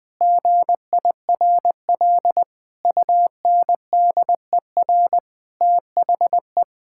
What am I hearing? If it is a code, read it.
Morse code, heard as a keyed tone, about 20 wpm, 700 Hz. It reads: GIRL UNDER THE
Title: Sentences from Top 1600 Words 20wpm